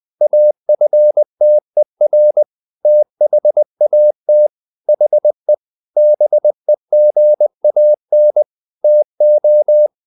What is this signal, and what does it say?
Morse code, heard as a keyed tone, about 20 wpm, 600 Hz. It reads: AFTER THAT HE BEGAN TO
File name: the old man and the sea part 2 20wpm